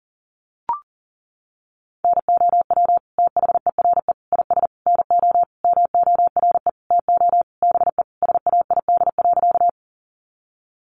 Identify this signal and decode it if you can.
Morse code, heard as a keyed tone, about 40 wpm, 700 Hz. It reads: NOW THERE IS NO MORE TO BE SAID.
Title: Sentences from Top 100 Words 40wpm